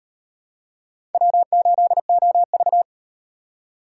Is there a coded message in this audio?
Yes: Morse code from 1.1 to 2.8 s, heard as a keyed tone, about 38 wpm, 700 Hz. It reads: W8OV